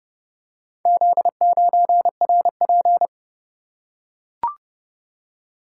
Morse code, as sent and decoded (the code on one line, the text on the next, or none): --.. ----. .-. .--.
Z9RP